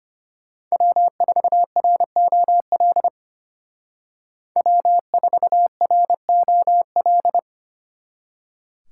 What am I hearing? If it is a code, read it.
Morse code, heard as a keyed tone, about 31 wpm, 700 Hz. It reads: W4ROL W4ROL
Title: Speed Racing - USA Call Signs 25wpm